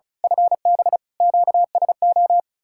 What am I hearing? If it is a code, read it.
Morse code, heard as a keyed tone, about 33 wpm, 700 Hz. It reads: FB QSO
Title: Speed-Racing Common QSO Phrases - 35wpm